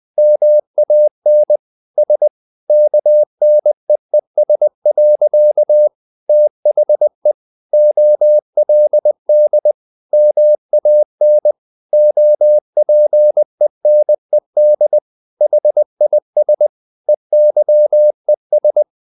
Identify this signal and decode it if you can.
Morse code, heard as a keyed tone, about 20 wpm, 600 Hz. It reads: MAN S KNEES. THE OLD MAN OPENED HIS EYES
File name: the old man and the sea / the old man and the sea part 1 20wpm